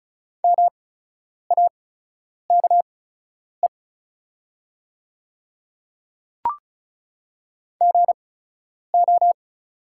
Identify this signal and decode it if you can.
Morse code, heard as a keyed tone, about 36 wpm, spaced as for 10 wpm, 700 Hz. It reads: MAKE GO
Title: 8x Character Spacing - Top 100 Words - 35wpm